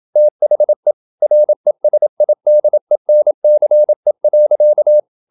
Morse code, heard as a keyed tone, about 27 wpm, 600 Hz. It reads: THE RESIDENCE.
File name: Never 0027wpm